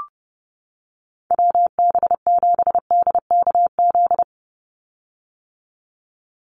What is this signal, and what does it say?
Morse code, heard as a keyed tone, about 30 wpm, 700 Hz. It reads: WB7DKZ
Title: Call Signs - 30wpm